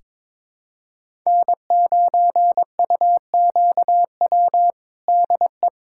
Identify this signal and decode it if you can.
Morse code, heard as a keyed tone, about 22 wpm, 700 Hz. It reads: N9UQW DE
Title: USA Callsign DE Callsign - 5 Characters - 22wpm